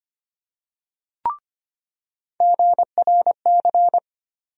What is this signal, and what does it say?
Morse code, heard as a keyed tone, about 25 wpm, 700 Hz. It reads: GRC
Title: USA Call Sign Postfixes - 3 Letters 25wpm